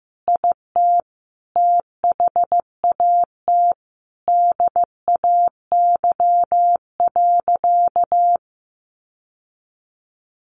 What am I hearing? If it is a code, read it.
Morse code, heard as a keyed tone, about 15 wpm, 700 Hz. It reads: IT THAT DAY.
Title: Sentences from Top 1500 Words 15wpm